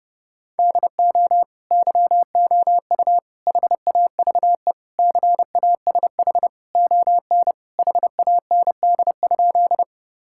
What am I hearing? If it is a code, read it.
Morse code, heard as a keyed tone, about 32 wpm, 700 Hz. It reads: DO YOU HAVE CASH ON HAND?
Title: Sentences from Top 1000 Words - Review Part 5 - 30wpm